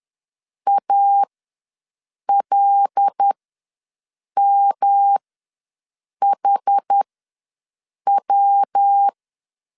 Morse code, heard as a keyed tone, about 10 wpm, spaced as for 6 wpm, 800 Hz. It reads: ALMHW
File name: LEZIONI / 5 LBVC / 5 L ws 10wpm 3636